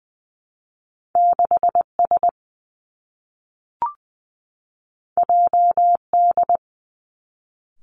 Morse code, heard as a keyed tone, about 20 wpm, 700 Hz. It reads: 6 S J D
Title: Two Challenging Letter-Number Combinations 20wpm